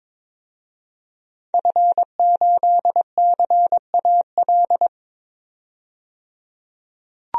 Morse code, heard as a keyed tone, about 22 wpm, 700 Hz. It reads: F8CAL
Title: DXCC-100 Call Signs - 5 Characters - 22wpm